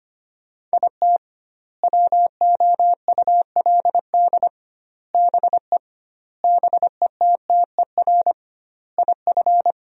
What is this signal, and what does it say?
Morse code, heard as a keyed tone, about 25 wpm, 700 Hz. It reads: IT WOULD BE BETTER IF
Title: 2x Word-Spacing - Sentences from Top 400 Words - 25wpm